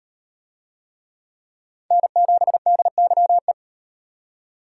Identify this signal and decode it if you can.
Morse code, heard as a keyed tone, about 38 wpm, 700 Hz. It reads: N7DYE